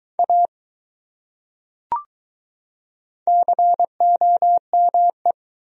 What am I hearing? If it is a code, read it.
Morse code, heard as a keyed tone, about 22 wpm, 700 Hz. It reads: A COME